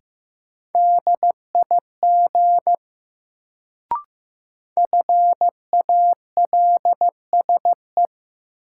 Morse code, heard as a keyed tone, about 15 wpm, 700 Hz. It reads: DIG FALSE
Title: Sentences from Top 1600 Words - New Word Drill - 15wpm